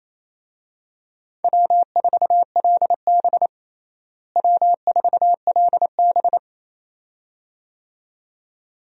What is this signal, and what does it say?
Morse code, heard as a keyed tone, about 28 wpm, 700 Hz. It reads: W4LB W4LB
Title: USA Call Signs - 4 Characters - Repeated 2x - 28wpm